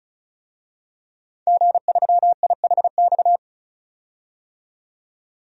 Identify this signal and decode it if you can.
Morse code, heard as a keyed tone, about 36 wpm, 700 Hz. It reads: G3IHX
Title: DXCC-100 Call Signs - 5 Characters - 35wpm